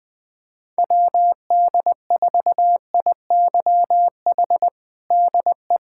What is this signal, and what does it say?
Morse code, heard as a keyed tone, about 20 wpm, 700 Hz. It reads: WD4IYH DE
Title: Practice QSO - Exchange 4 - Ending - 20wpm